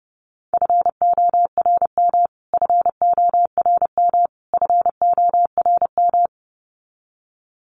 Morse code, heard as a keyed tone, about 30 wpm, 700 Hz. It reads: FORM FORM FORM
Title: Top 500 Words Repeated 30wpm